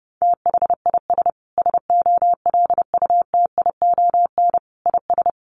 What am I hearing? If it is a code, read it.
Morse code, heard as a keyed tone, about 30 wpm, 700 Hz. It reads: THIS SOLUTION IS
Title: Sentences from Top 1300 Words 30wpm